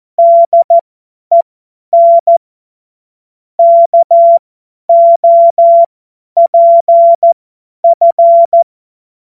Morse code, heard as a keyed tone, about 14 wpm, spaced as for 10 wpm, 680 Hz. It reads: DEN KOPF
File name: Morsekurs / Witze / Witz 02. Dezember 2025 50 F70